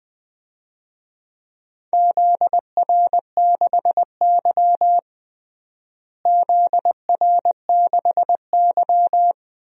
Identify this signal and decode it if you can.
Morse code, heard as a keyed tone, about 21 wpm, 700 Hz. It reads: ZR6Y ZR6Y
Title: DXCC-100 Call Signs - 4 Characters - Repeated 2x - 20wpm